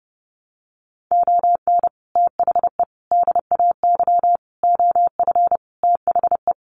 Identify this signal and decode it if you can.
Morse code, heard as a keyed tone, about 30 wpm, 700 Hz. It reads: ON THE DAY OF THE
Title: Sets of 5 Words Encore 30wpm